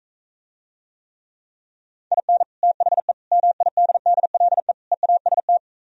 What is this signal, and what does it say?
Morse code, heard as a keyed tone, about 44 wpm, 700 Hz. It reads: IN THE MIDDLE EAST